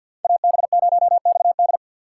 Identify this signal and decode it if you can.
Morse code, heard as a keyed tone, about 47 wpm, 700 Hz. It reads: AB0XD
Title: USA Call Signs - 5 Characters - 4x Character Spacing - 50wpm